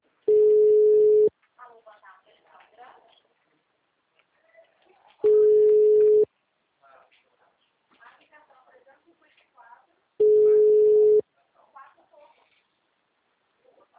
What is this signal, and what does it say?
Ringback tone (Europe, 425 Hz), with speech.